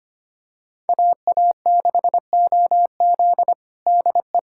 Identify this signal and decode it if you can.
Morse code, heard as a keyed tone, about 25 wpm, 700 Hz. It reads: AA6OZ DE
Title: USA Callsign DE Callsign 25wpm